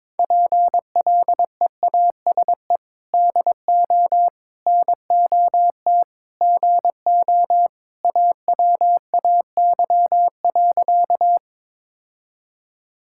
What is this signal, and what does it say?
Morse code, heard as a keyed tone, about 22 wpm, 700 Hz. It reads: PLEASE DO NOT GO AWAY.
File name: practice / 22wpm / Sentences with 4-Word Phrases - 22wpm